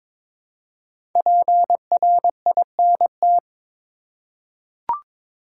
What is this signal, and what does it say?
Morse code, heard as a keyed tone, about 23 wpm, 700 Hz. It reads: PRINT
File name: 5 Letter Words - First 3 Letters Given - 22wpm